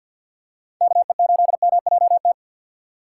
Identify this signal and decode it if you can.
Morse code, heard as a keyed tone, about 50 wpm, 700 Hz. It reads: KE8MJT
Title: USA Call Signs - 2x Character Spacing - 50wpm